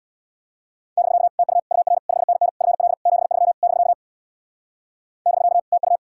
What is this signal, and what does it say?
Morse code, heard as a keyed tone, about 65 wpm, 700 Hz. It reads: <BK> TU UR 5NN FL 73 <BK> <BK> TU